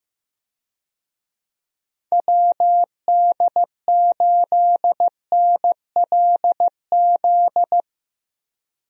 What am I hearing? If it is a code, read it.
Morse code, heard as a keyed tone, about 16 wpm, 700 Hz. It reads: WD8NLZ
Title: USA Call Signs - 6 Characters - Repeated 2x - 15wpm